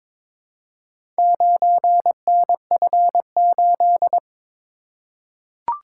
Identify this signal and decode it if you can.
Morse code, heard as a keyed tone, about 22 wpm, 700 Hz. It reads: 9NF8
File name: practice / 22wpm / Four Letter-Number Combinations - 22wpm